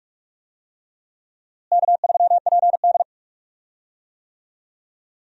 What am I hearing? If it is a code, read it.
Morse code, heard as a keyed tone, about 46 wpm, 700 Hz. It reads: K3PD
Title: USA Call Signs - 4 Characters - Repeated 3x - 45wpm